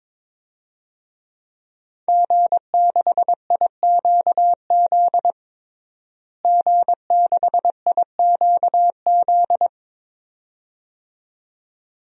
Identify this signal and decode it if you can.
Morse code, heard as a keyed tone, about 22 wpm, 700 Hz. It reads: G6IQZ G6IQZ
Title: DXCC-100 Call Signs - 5 Characters - Repeated 2x - 22wpm